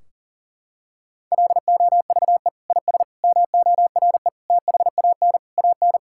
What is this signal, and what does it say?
Morse code, heard as a keyed tone, about 40 wpm, 700 Hz. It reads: LOVE IS MORE THAN AN
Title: Sentences from Top 1600 Words 40wpm